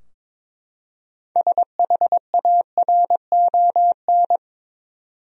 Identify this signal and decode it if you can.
Morse code, heard as a keyed tone, about 22 wpm, 700 Hz. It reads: SHARON
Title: USA Common Names - 22wpm